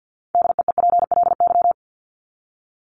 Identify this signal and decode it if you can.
Morse code, heard as a keyed tone, about 50 wpm, 700 Hz. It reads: DEEPLY
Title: LY Suffix - Top 100 Words with Base Words 50wpm